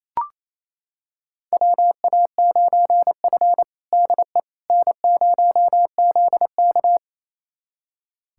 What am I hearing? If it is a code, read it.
Morse code, heard as a keyed tone, about 28 wpm, 700 Hz. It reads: WA9F DE N0ZK
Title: USA Callsign DE Callsign - 4 Characters - 28wpm